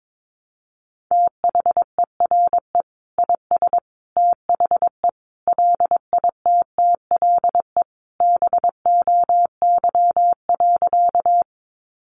Morse code, heard as a keyed tone, about 22 wpm, 700 Hz. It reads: THERE IS THE LITTLE BOY.
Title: Sentences from Top 200 Words 22wpm